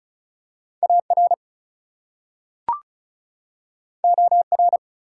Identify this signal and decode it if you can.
Morse code, heard as a keyed tone, about 36 wpm, 700 Hz. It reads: AR OR
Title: US States and Canadian Provinces 35wpm